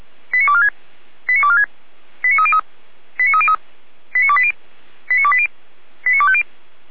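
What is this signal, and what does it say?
Five-tone selective call (ZVEI), heard as quick runs of five tones, tones of about 70 ms. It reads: 89246 89246 89303 89303 89280 89280 89250
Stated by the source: Five-tone ZVEI selective call received on an SDR, recorded by fire brigade Feuerwehr Orlamünde.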